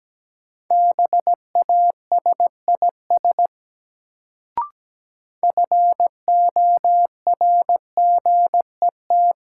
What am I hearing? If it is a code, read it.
Morse code, heard as a keyed tone, about 17 wpm, 700 Hz. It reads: BASIS FORGET
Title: Sentences from Top 1200 Words - New Word Drill Alt - 17wpm